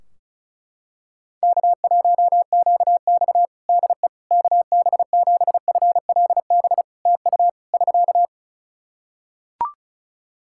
Morse code, heard as a keyed tone, about 36 wpm, 700 Hz. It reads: K1QX DE KB7FLB TU <SK>
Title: Practice QSO - Exchange 4 - Ending - 35wpm